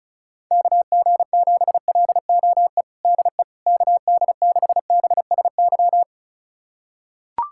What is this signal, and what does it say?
Morse code, heard as a keyed tone, about 34 wpm, 700 Hz. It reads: KG7LOE DE KD6BSY
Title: USA Callsign DE Callsign - 6 Characters - 35wpm